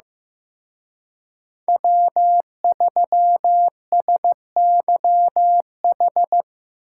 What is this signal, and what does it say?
Morse code, heard as a keyed tone, about 18 wpm, 700 Hz. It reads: W3SYH
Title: Speed Racing - USA 1x3 Call Signs 15wpm